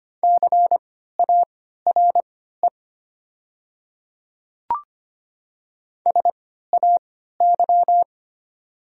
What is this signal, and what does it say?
Morse code, heard as a keyed tone, about 25 wpm, spaced as for 14 wpm, 700 Hz. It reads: CARE SAY
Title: 3x Character Spacing - Top 300 Words - 25wpm